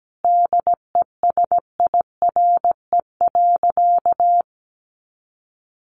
Morse code, heard as a keyed tone, about 17 wpm, 700 Hz. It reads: DESIRE.
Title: Sentences from Top 1100 Words 17wpm